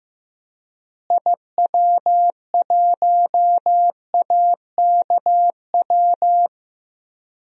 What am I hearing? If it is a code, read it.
Morse code, heard as a keyed tone, about 15 wpm, 700 Hz. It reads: IW1AKW
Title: DXCC-100 Call Signs - 6 Characters - Repeated 3x - 15wpm